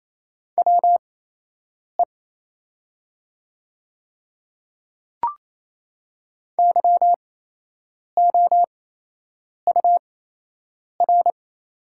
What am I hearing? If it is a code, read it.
Morse code, heard as a keyed tone, about 28 wpm, spaced as for 8 wpm, 700 Hz. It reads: WE YOUR
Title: 8x Character Spacing - Top 300 Words - 28wpm